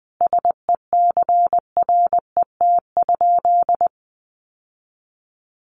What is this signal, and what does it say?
Morse code, heard as a keyed tone, about 20 wpm, 700 Hz. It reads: SECRET?
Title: Sentences from Top 1200 Words 20wpm